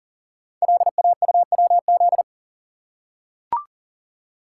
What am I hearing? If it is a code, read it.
Morse code, heard as a keyed tone, about 39 wpm, 690 Hz. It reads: LAUWZ
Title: Code Groups - 5 Letter:Numbers - 40wpm